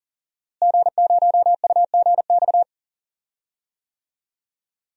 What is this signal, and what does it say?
Morse code, heard as a keyed tone, about 40 wpm, 700 Hz. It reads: G0UGX
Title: DXCC-100 Call Signs - 5 Characters - Repeated 2x - 40wpm